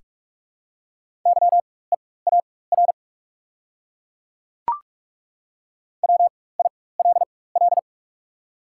Morse code, heard as a keyed tone, about 42 wpm, spaced as for 21 wpm, 690 Hz. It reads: YEAR WILL
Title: 4x Character Spacing - Top 200 Words - 45wpm